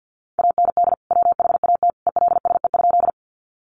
Morse code, heard as a keyed tone, about 50 wpm, 700 Hz. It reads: AND WHAT ELSE?
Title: Sentences from Top 800 Words 50wpm